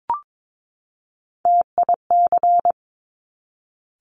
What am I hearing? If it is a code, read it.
Morse code, heard as a keyed tone, about 22 wpm, 700 Hz. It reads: TIC
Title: Top Three Letter Combinations in Words 22wpm